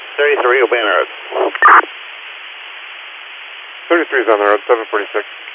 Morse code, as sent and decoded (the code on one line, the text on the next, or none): none